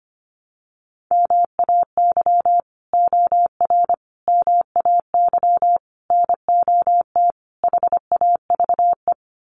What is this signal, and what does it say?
Morse code, heard as a keyed tone, about 25 wpm, 700 Hz. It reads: MAY OR MAY NOT HAVE
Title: Sets of 5 Words Encore2 25wpm